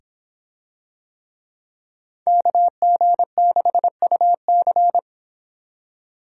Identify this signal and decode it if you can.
Morse code, heard as a keyed tone, about 24 wpm, 700 Hz. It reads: KG6UC